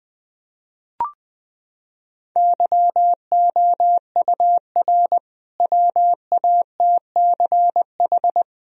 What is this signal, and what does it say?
Morse code, heard as a keyed tone, about 21 wpm, 700 Hz. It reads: YOUR WATCH
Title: Sentences from Top 1000 Words - Review Part 2 - 20wpm